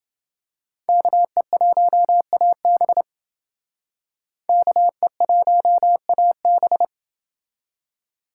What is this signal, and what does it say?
Morse code, heard as a keyed tone, about 29 wpm, 700 Hz. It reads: KE1AB KE1AB